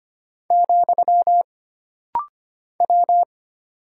Morse code, heard as a keyed tone, about 25 wpm, 700 Hz. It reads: , W
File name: Single Character (Letter:Number:Punctuation:Prosign) - Rapid-Fire - 25wpm